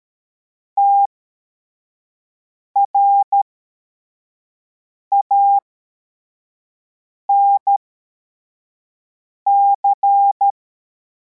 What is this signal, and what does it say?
Morse code, heard as a keyed tone, about 13 wpm, spaced as for 4 wpm, 800 Hz. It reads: TRANC
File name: formation / cw / Text42